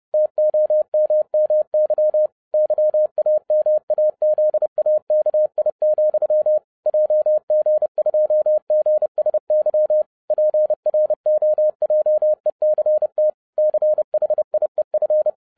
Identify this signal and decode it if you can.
Morse code, heard as a keyed tone, about 30 wpm, 610 Hz. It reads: TOMMY YAMAZAKI, JG2GSY PROJECT CHIEF